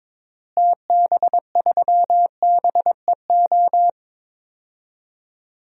Morse code, heard as a keyed tone, about 21 wpm, 700 Hz. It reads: TB3BEO
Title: DXCC-100 Call Signs - 6 Characters - Repeated 2x - 22wpm